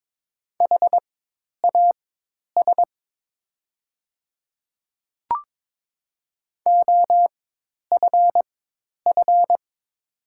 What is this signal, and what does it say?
Morse code, heard as a keyed tone, about 22 wpm, spaced as for 10 wpm, 700 Hz. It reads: HAS OFF
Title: 4x Character Spacing - Top 200 Words - 22wpm